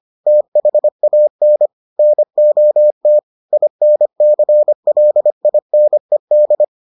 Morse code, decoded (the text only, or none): THAN NOT INCLINED